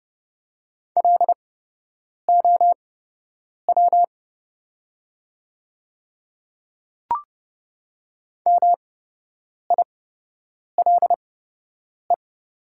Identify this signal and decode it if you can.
Morse code, heard as a keyed tone, about 30 wpm, spaced as for 8 wpm, 700 Hz. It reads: LOW MILE
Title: 8x Character Spacing - Top 300 Words - 30wpm